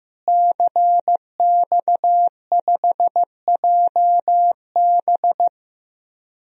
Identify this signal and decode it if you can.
Morse code, heard as a keyed tone, about 15 wpm, 700 Hz. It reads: CX5JB
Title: DXCC-100 Call Signs - 5 Characters - Repeated 3x - 15wpm